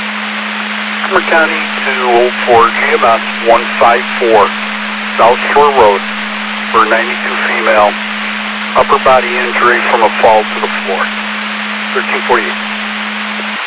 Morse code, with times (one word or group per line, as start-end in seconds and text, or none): none